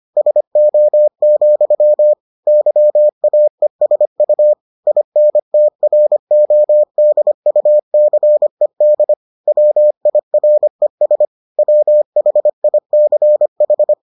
Morse code, heard as a keyed tone, about 25 wpm, 600 Hz. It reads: SO, YAESU INTRODUCED WIRES WHICH